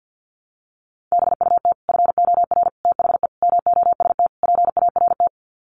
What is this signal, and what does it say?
Morse code, heard as a keyed tone, about 50 wpm, 700 Hz. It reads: BUT FOR THE MOST PART
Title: Sets of 5 Words Encore 50wpm